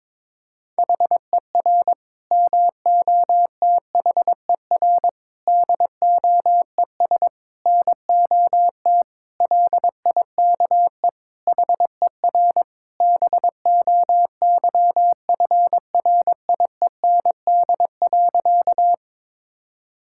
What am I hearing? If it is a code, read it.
Morse code, heard as a keyed tone, about 22 wpm, 700 Hz. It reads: HER MOTHER DOES NOT LIKE HER BOYFRIEND.
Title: Sentences from Top 1000 Words - Review Part 5 - 22wpm